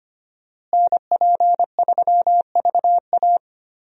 Morse code, heard as a keyed tone, about 25 wpm, 700 Hz. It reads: NP3VA
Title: USA Call Signs - 5 Characters - Repeated 2x - 25wpm